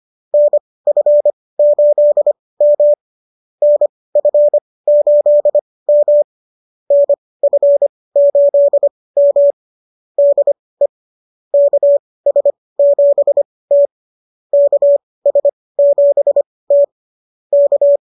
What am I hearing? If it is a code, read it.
Morse code, heard as a keyed tone, about 25 wpm, spaced as for 18 wpm, 590 Hz. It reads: NF8M NF8M NF8M DE KS7T KS7T K